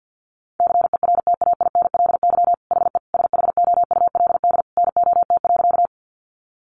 Morse code, heard as a keyed tone, about 50 wpm, 700 Hz. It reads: CERTAINLY HE SHOULD NOT.
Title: Sentences from Top 900 Words 50wpm